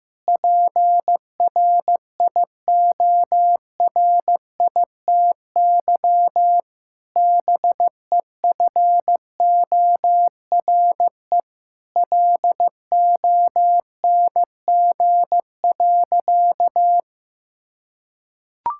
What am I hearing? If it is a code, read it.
Morse code, heard as a keyed tone, about 15 wpm, 700 Hz. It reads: PRIORITY BEFORE LONG.
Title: Sentences from Top 1600 Words 15wpm